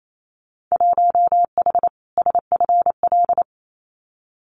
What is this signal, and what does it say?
Morse code, heard as a keyed tone, about 28 wpm, 700 Hz. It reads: 1H SFL
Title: Winter Field Day Exchanges 28wpm